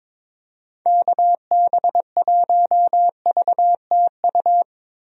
Morse code, heard as a keyed tone, about 22 wpm, 700 Hz. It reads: KB1VTU
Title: USA Call Signs - 6 Characters - Repeated 3x - 22wpm